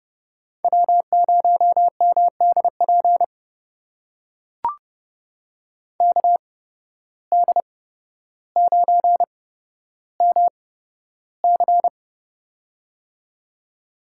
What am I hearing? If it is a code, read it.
Morse code, heard as a keyed tone, about 29 wpm, spaced as for 13 wpm, 700 Hz. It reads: W0MDP KD9MC
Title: USA Call Signs - 5 Characters - 8x Character Spacing - 30wpm